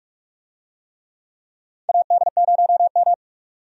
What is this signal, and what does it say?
Morse code, heard as a keyed tone, about 46 wpm, 700 Hz. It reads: AD0M
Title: USA Call Signs - 4 Characters - 45wpm